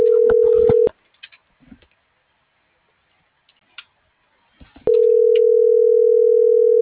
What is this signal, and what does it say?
Ringback tone (North America), 440 + 480 Hz.